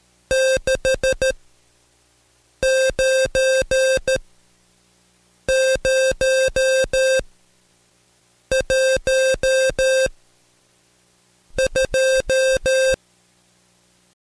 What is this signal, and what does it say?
Morse code, heard as a keyed tone, about 13 wpm, spaced as for 5 wpm, 530 Hz. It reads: 69012
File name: Code Quick / AudioDownld2 / Seg107-Day13-DayDrill 13